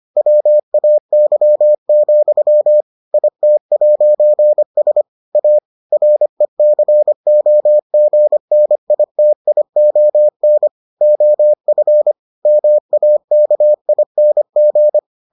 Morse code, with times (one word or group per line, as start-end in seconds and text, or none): 0.2-2.8 WAY,
3.1-5.0 IT'S
5.3-5.6 A
5.9-10.7 RECOGNITION
11.0-12.1 OF
12.5-15.0 MAKING